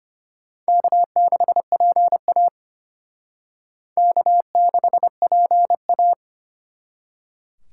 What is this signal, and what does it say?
Morse code, heard as a keyed tone, about 31 wpm, 700 Hz. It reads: K6PA K6PA
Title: Speed Racing - USA Call Signs 25wpm